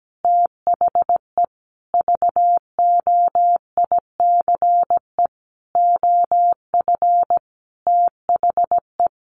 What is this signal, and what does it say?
Morse code, heard as a keyed tone, about 17 wpm, 700 Hz. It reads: THE VOICE OF THE
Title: Sentences from Top 1500 Words 17wpm